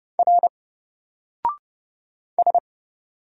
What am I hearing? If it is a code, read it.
Morse code, heard as a keyed tone, about 30 wpm, 700 Hz. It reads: R S
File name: Single-Letter-Number-30wpm